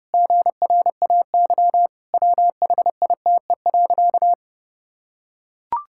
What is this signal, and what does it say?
Morse code, heard as a keyed tone, about 30 wpm, 700 Hz. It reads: GRAY WHITE.
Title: Pangram Challenge v2 - 30wpm